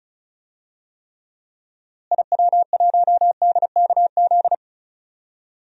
Morse code, heard as a keyed tone, about 34 wpm, 700 Hz. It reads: IW1DKZ
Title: DXCC-100 Call Signs - 6 Characters - Repeated 3x - 35wpm